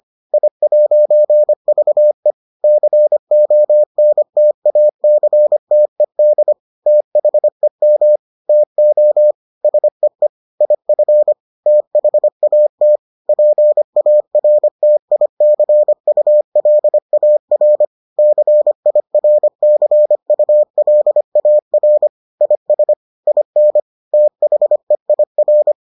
Morse code, heard as a keyed tone, about 25 wpm, 600 Hz. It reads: I'VE CONTACTED THEM TO SEE IF THAT PARTICULAR CIRCULAR IS IN THEIR